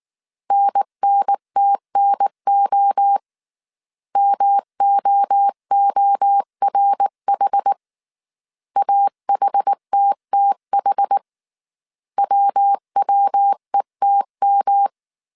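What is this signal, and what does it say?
Morse code, heard as a keyed tone, about 19 wpm, 800 Hz. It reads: NNTNO MOORH AHTTH WWETM
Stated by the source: Morse code dictation at 18 wpm from the IW3SGT QRS Morse code training course, five-character groups of 300 characters.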